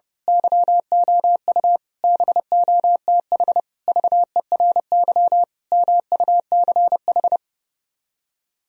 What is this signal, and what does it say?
Morse code, heard as a keyed tone, about 28 wpm, 700 Hz. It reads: YOU BOTH VERY MUCH
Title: Speed-Racing - Sets of 5 Words 30wpm